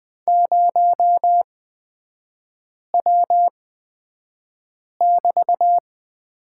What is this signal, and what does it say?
Morse code, heard as a keyed tone, about 20 wpm, spaced as for 5 wpm, 700 Hz. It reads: 0W<BT>
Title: 250 Lesson 40e - BK - Three Characters 20wpm